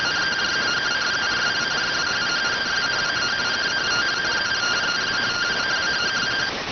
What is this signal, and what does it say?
Two-tone teleprinter signal, about 170 Hz shift.